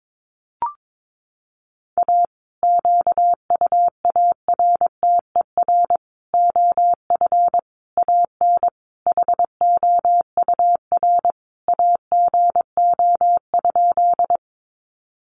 Morse code, heard as a keyed tone, about 22 wpm, 700 Hz. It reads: A QUARTER OF AN HOUR AGO?
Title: Sentences from Top 1300 Words 22wpm